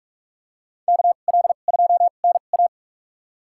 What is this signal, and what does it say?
Morse code, heard as a keyed tone, about 45 wpm, spaced as for 32 wpm, 690 Hz. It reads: KL2NA
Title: USA Call Signs - 5 Characters - 2x Character Spacing - 45wpm